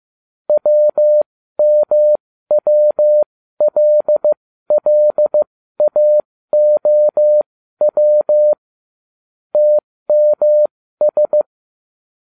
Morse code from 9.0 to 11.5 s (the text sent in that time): TMS